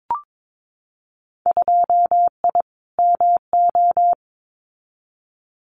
Morse code, heard as a keyed tone, about 22 wpm, 700 Hz. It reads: 2I MO
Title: Winter Field Day Exchanges 22wpm